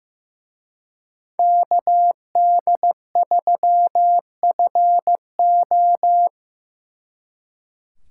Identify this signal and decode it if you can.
Morse code, heard as a keyed tone, about 18 wpm, 700 Hz. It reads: KD3FO
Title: Speed Racing - USA 2x2 Call Signs 15wpm